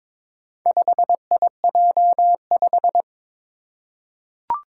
Morse code, heard as a keyed tone, about 22 wpm, 700 Hz. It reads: 5IJ5
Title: Four Challenging Letter-Number Combinations 22wpm